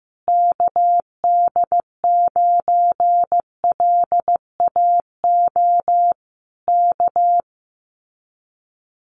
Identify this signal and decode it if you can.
Morse code, heard as a keyed tone, about 15 wpm, 700 Hz. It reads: KD9LAO K
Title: Common QSO Phrases Encore 15wpm